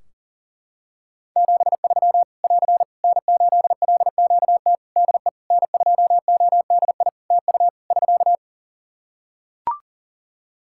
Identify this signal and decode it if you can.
Morse code, heard as a keyed tone, about 40 wpm, 700 Hz. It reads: 73 <AR> N8LQT DE N2ODI TU <SK>
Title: K1USN SST QSO Practice 40wpm